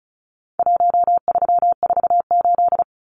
Morse code, heard as a keyed tone, about 34 wpm, 700 Hz. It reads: 1348
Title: Numbers - Contest Style - 4-Digit 35wpm